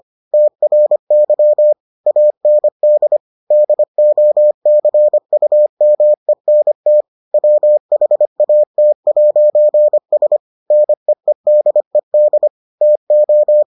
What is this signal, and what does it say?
Morse code, heard as a keyed tone, about 25 wpm, 600 Hz. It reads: TRY AND DOCUMENT WHAT'S NEEDED TO